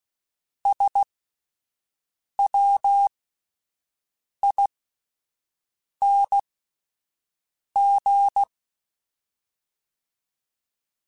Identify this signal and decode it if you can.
Morse code, heard as a keyed tone, about 16 wpm, spaced as for 5 wpm, 800 Hz. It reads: SWING
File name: Misc / Misc-mp3s / Words @5